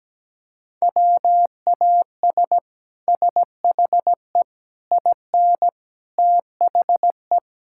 Morse code, heard as a keyed tone, about 17 wpm, 700 Hz. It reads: WAS SHE IN THE
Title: Sentences from Top 1000 Words - Review Part 5 - 17wpm